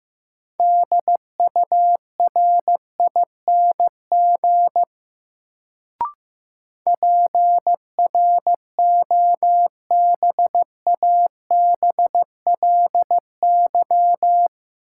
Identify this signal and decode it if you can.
Morse code, heard as a keyed tone, about 15 wpm, 700 Hz. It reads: DURING PROBABLY
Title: Sentences from Top 700 Words - New Word Drill - 15wpm